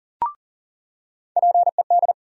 Morse code, heard as a keyed tone, about 40 wpm, 700 Hz. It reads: PED
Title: USA Call Sign Postfixes - 3 Letters 40wpm